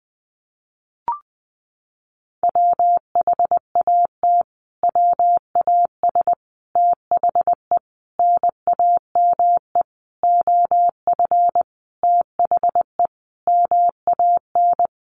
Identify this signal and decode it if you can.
Morse code, heard as a keyed tone, about 20 wpm, 700 Hz. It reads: WHAT WAS THE NAME OF THE MAN
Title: Sentences from Top 300 Words 20wpm